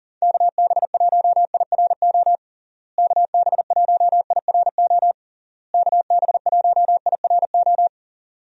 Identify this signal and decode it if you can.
Morse code, heard as a keyed tone, about 40 wpm, 700 Hz. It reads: KB1IRO KB1IRO KB1IRO
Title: USA Call Signs - 6 Characters - Repeated 3x - 40wpm